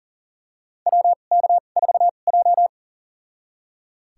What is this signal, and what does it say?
Morse code, heard as a keyed tone, about 39 wpm, spaced as for 29 wpm, 700 Hz. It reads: WK4J
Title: USA Call Signs - 4 Characters - 2x Character Spacing - 40wpm